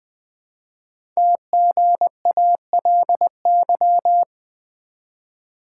Morse code, heard as a keyed tone, about 20 wpm, 700 Hz. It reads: TGALY
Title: Code Groups - 5 Letters - 20wpm